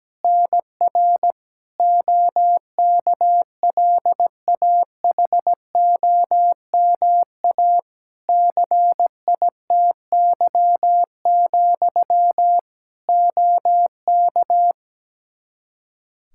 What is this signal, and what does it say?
Morse code, heard as a keyed tone, about 17 wpm, 700 Hz. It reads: NR OKLAHOMA CITY, OK
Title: Practice QSO - Exchange 1 - Beginning - 17wpm